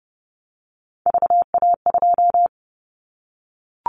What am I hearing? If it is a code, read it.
Morse code, heard as a keyed tone, about 30 wpm, 700 Hz. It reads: VA2
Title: Callsign Prefixes for Top 50 DXCC - 30wpm